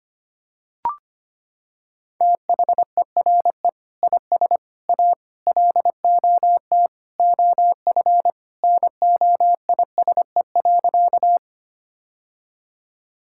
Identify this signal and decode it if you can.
Morse code, heard as a keyed tone, about 25 wpm, 700 Hz. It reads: THERE IS A LOT OF NOISE.
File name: practice / 25wpm / Sentences with 2-Word Phrases - 25wpm